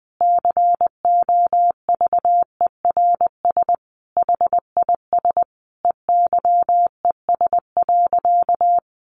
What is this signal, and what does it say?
Morse code, heard as a keyed tone, about 20 wpm, 700 Hz. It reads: COVERS HIS EYES.
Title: Easy Phrases 20wpm